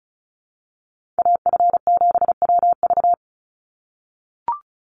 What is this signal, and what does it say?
Morse code, heard as a keyed tone, about 36 wpm, 700 Hz. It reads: AF7WV
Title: USA Call Signs 35wpm